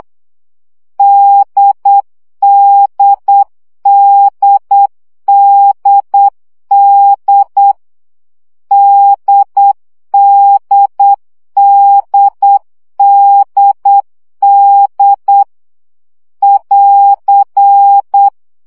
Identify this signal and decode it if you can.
Morse code, heard as a keyed tone, about 8 wpm, 800 Hz. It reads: DDDDD DDDDD <AR>
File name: CW gyakorlas d 7 baud